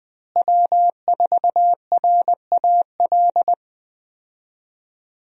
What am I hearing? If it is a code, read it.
Morse code, heard as a keyed tone, about 20 wpm, 700 Hz. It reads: W4RAL
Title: USA Call Signs - 5 Characters - Repeated 2x - 20wpm